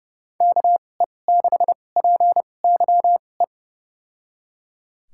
Morse code, heard as a keyed tone, about 30 wpm, spaced as for 22 wpm, 700 Hz. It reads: KE6PYE
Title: USA Call Signs - 2x Character Spacing - 30wpm